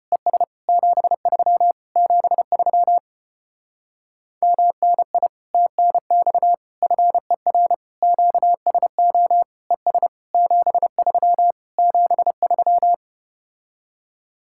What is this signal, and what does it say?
Morse code, heard as a keyed tone, about 33 wpm, 700 Hz. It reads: ES 73 73 MNI TNX FER QSO ES 73 73